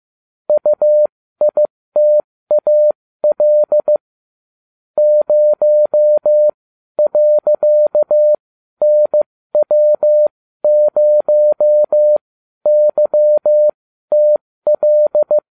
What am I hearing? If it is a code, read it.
Morse code, heard as a keyed tone, about 15 wpm, spaced as for 12 wpm, 600 Hz. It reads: UITAL 0.NW0YTL